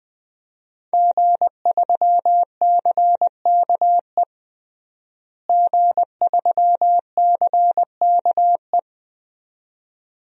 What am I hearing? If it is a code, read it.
Morse code, heard as a keyed tone, about 21 wpm, 700 Hz. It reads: G3CKE G3CKE
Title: DXCC-100 Call Signs - 5 Characters - Repeated 2x - 20wpm